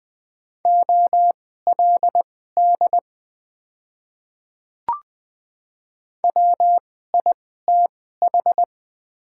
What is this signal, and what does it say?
Morse code, heard as a keyed tone, about 20 wpm, spaced as for 15 wpm, 700 Hz. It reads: OLD WITH
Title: 2x Character Spacing - Top 300 Words - 20wpm